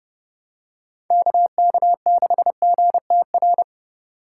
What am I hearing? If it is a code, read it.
Morse code, heard as a keyed tone, about 30 wpm, 700 Hz. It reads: KK6GTR